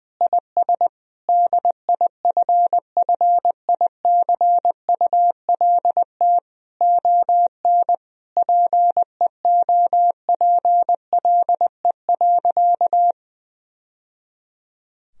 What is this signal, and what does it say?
Morse code, heard as a keyed tone, about 20 wpm, 700 Hz. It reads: IS DIFFICULT ON PEOPLE.
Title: Sentences from Top 1000 Words - Review Part 3 - 20wpm